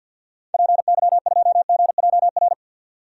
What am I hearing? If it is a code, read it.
Morse code, heard as a keyed tone, about 49 wpm, 700 Hz. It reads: PY2GJR
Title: DXCC-100 Call Signs - 6 Characters - 50wpm